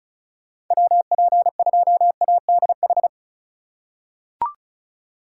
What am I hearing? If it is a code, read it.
Morse code, heard as a keyed tone, about 34 wpm, 700 Hz. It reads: WP2ADH
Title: USA Call Signs - 6 Characters - 4x Character Spacing - 35wpm